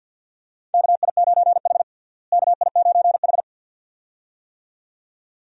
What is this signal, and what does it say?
Morse code, heard as a keyed tone, about 47 wpm, 700 Hz. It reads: KI9H KI9H
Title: USA Call Signs - 4 Characters - Repeated 2x - 50wpm